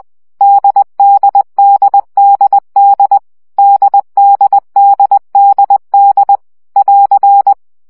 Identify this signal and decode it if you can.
Morse code, heard as a keyed tone, about 20 wpm, 800 Hz. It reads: DDDDD DDDDD <AR>
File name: CW gyakorlas d 17 baud